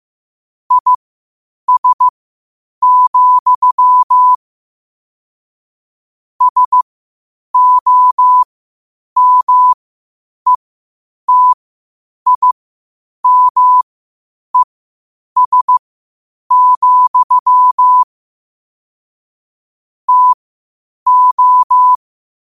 Morse code, heard as a keyed tone, about 15 wpm, spaced as for 8 wpm, 1000 Hz. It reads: IS, SOMETIMES, TO
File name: qotdfiles / 8wpm